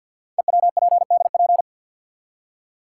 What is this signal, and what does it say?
Morse code, heard as a keyed tone, about 46 wpm, 700 Hz. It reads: EWPDP
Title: Code Groups - 5 Letters - 50wpm